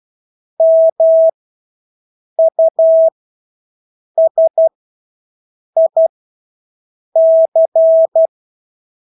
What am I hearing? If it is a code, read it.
Morse code, heard as a keyed tone, about 12 wpm, spaced as for 6 wpm, 650 Hz. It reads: MUSIC
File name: Engelske ord 12-6 wpm